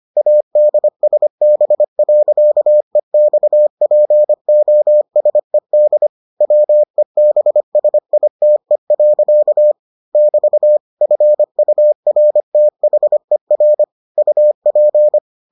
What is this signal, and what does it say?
Morse code, heard as a keyed tone, about 25 wpm, 600 Hz. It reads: ADSB.EXPOSED WEBSITE. <BT> FURTHER UP